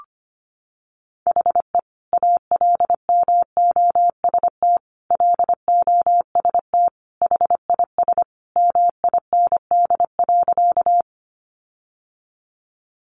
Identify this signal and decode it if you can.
Morse code, heard as a keyed tone, about 25 wpm, 700 Hz. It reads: HE ALMOST LOST HIS MIND.
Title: Sentences from Top 1400 Words 25wpm